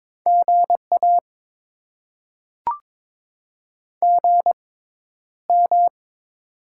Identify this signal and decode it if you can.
Morse code, heard as a keyed tone, about 21 wpm, spaced as for 11 wpm, 700 Hz. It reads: GA GM
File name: Top 20 QSO Elements - 6x Character Spacing - 22wpm